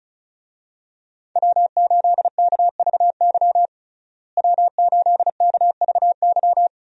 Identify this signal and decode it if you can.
Morse code, heard as a keyed tone, about 34 wpm, 700 Hz. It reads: W8KVY W8KVY
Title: USA Call Signs - 5 Characters - Repeated 2x - 35wpm